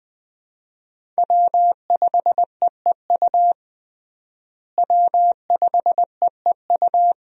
Morse code, heard as a keyed tone, about 20 wpm, 700 Hz. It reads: W5EEU W5EEU
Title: USA Call Signs - 5 Characters - Repeated 2x - 20wpm